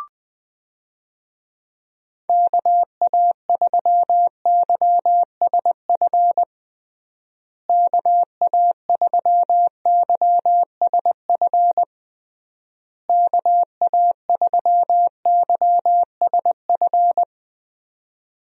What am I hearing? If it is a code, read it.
Morse code, heard as a keyed tone, about 20 wpm, 700 Hz. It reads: KA3YSF KA3YSF KA3YSF
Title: USA Call Signs - 6 Characters - Repeated 3x - 20wpm